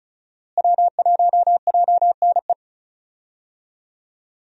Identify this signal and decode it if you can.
Morse code, heard as a keyed tone, about 34 wpm, 700 Hz. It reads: W1JNE
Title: USA Call Signs - 5 Characters - Repeated 3x - 35wpm